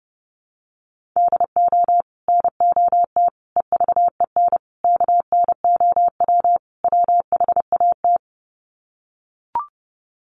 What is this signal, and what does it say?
Morse code, heard as a keyed tone, about 30 wpm, 700 Hz. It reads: DO NOT EVEN KNOW WHAT
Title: Sets of 5 Words Encore 30wpm